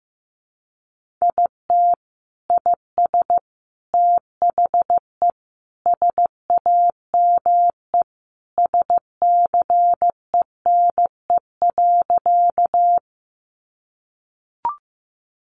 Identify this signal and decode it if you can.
Morse code, heard as a keyed tone, about 15 wpm, 700 Hz. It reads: IT IS THE SAME SCENE.
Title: Sentences from Top 1100 Words 15wpm